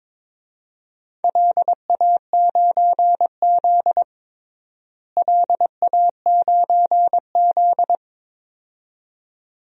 Morse code, heard as a keyed tone, about 21 wpm, 700 Hz. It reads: LA9Z LA9Z
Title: DXCC-100 Call Signs - 4 Characters - Repeated 2x - 22wpm